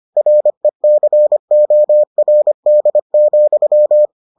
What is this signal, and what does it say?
Morse code, heard as a keyed tone, about 25 wpm, 600 Hz. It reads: RECORD,